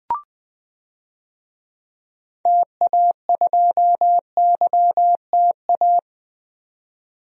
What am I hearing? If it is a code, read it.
Morse code, heard as a keyed tone, about 20 wpm, 700 Hz. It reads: TA2YTA
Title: DXCC-100 Call Signs - 6 Characters - 20wpm